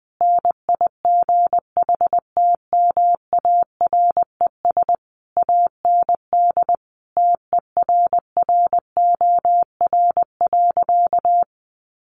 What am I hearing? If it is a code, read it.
Morse code, heard as a keyed tone, about 19 wpm, 700 Hz. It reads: NIGHTMARES AND TERROR.
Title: A Sound Of Thunder 20wpm